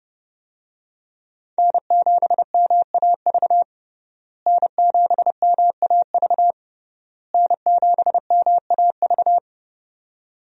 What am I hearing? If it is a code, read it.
Morse code, heard as a keyed tone, about 30 wpm, 700 Hz. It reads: N7MAV N7MAV N7MAV
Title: USA Call Signs - 5 Characters - Repeated 3x - 30wpm